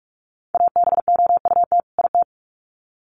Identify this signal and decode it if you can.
Morse code, heard as a keyed tone, about 44 wpm, 700 Hz. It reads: ABOUT IT
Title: Sets of 2 Words 45wpm